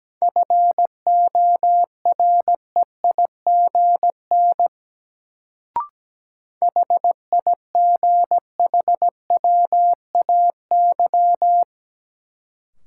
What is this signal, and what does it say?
Morse code, heard as a keyed tone, about 17 wpm, 700 Hz. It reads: FOREIGN HIGHWAY
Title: Sentences from Top 1400 Words - New Word Drill Alt - 17wpm